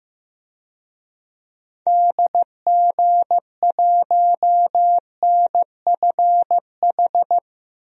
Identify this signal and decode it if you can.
Morse code, heard as a keyed tone, about 15 wpm, 700 Hz. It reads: DG1NFH
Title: DXCC-100 Call Signs - 6 Characters - Repeated 3x - 15wpm